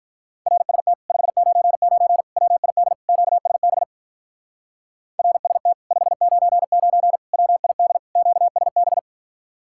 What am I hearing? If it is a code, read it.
Morse code, heard as a keyed tone, about 43 wpm, 700 Hz. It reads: RST 599 WID QSB RST 599 WID QSB
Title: Speed-Racing Common QSO Phrases Encore - 35wpm